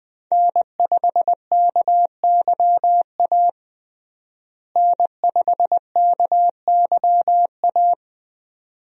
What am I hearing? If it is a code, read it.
Morse code, heard as a keyed tone, about 20 wpm, 700 Hz. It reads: N5KYA N5KYA
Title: USA Call Signs - 5 Characters - Repeated 2x - 20wpm